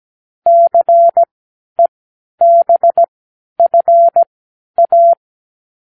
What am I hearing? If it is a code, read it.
Morse code, heard as a keyed tone, about 17 wpm, spaced as for 10 wpm, 680 Hz. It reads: CEBFA
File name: k7qo / Lesson9(A-F)